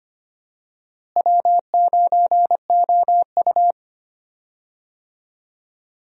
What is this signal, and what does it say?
Morse code, heard as a keyed tone, about 25 wpm, 700 Hz. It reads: W9OU
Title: USA Call Signs - 4 Characters - Repeated 2x - 25wpm